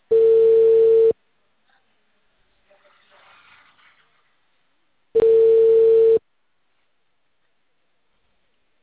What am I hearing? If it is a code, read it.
Ringback tone (China, 450 Hz).
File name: out- -2008- - - .21483